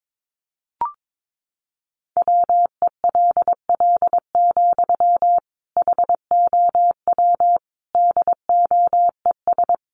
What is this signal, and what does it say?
Morse code, heard as a keyed tone, about 22 wpm, 700 Hz. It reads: WELL, HOW DOES
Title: Sentences from Top 1100 Words 22wpm